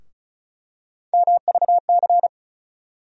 Morse code, heard as a keyed tone, about 34 wpm, 700 Hz. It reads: MVC
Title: USA Call Sign Postfixes - 3 Letters 35wpm